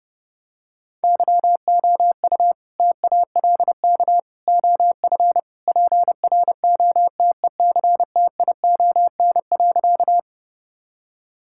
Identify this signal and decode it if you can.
Morse code, heard as a keyed tone, about 30 wpm, 700 Hz. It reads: YOU TALK OF PROTECTION.
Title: Sentences from Top 1600 Words 30wpm